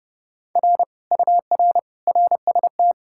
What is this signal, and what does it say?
Morse code, heard as a keyed tone, about 37 wpm, 700 Hz. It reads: R UR RST
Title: Speed-Racing Common QSO Phrases - 30wpm